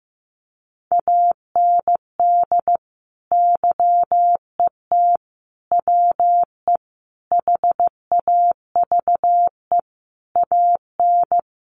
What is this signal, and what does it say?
Morse code, heard as a keyed tone, about 15 wpm, 700 Hz. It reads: AND YET WE HAVE AN
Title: Sentences from Top 1500 Words 15wpm